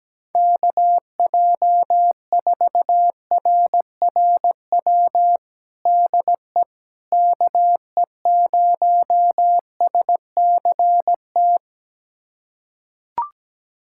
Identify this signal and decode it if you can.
Morse code, heard as a keyed tone, about 17 wpm, 700 Hz. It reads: KJ4RRW DE KE0SCT
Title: USA Callsign DE Callsign - 6 Characters - 17wpm